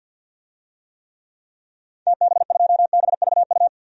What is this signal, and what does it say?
Morse code, heard as a keyed tone, about 47 wpm, 700 Hz. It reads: TB2BVU
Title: DXCC-100 Call Signs - 6 Characters - 50wpm